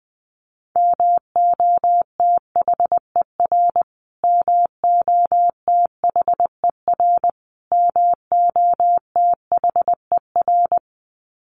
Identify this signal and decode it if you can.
Morse code, heard as a keyed tone, about 20 wpm, 700 Hz. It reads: MOTHER MOTHER MOTHER
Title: Top 200 Words Repeated 20wpm